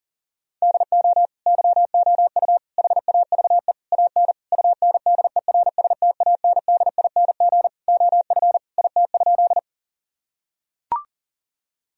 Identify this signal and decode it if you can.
Morse code, heard as a keyed tone, about 40 wpm, 700 Hz. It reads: DO YOU HAVE AN UNDERSTANDING OF IT?
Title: Sentences from Top 1000 Words - Review Part 3 - 40wpm